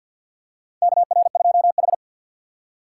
Morse code, heard as a keyed tone, about 51 wpm, 700 Hz. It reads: KR2H
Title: USA Call Signs - 4 Characters - 3x Character Spacing - 50wpm